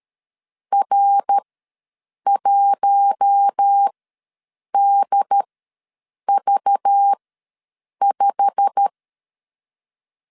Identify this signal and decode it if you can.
Morse code, heard as a keyed tone, about 13 wpm, spaced as for 7 wpm, 800 Hz. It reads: R1DV5
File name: LEZIONI / 8 YJ45 / 8 YJ45 ws 12wpm 3146